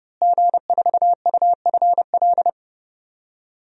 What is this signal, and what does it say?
Morse code, heard as a keyed tone, about 32 wpm, 700 Hz. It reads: G4UFL
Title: DXCC-100 Call Signs - 5 Characters - Repeated 3x - 30wpm